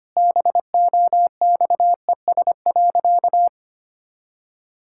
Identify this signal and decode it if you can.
Morse code, heard as a keyed tone, about 25 wpm, 700 Hz. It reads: BOXES.
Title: Pangram Challenge v2 - 25wpm